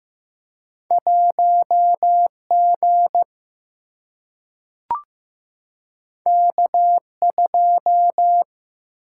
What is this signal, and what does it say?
Morse code, heard as a keyed tone, about 15 wpm, 700 Hz. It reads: 1 G K 2
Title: Single-Letter Single-Number 15wpm